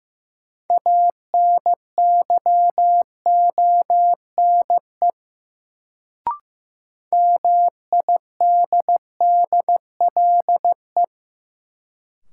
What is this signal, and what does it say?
Morse code, heard as a keyed tone, about 15 wpm, 700 Hz. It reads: ANYONE MIDDLE
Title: Sentences from Top 900 Words - New Word Drill - 15wpm